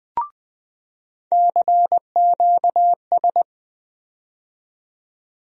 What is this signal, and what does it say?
Morse code, heard as a keyed tone, about 21 wpm, 700 Hz. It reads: CQS
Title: USA Call Sign Postfixes - 3 Letters 20wpm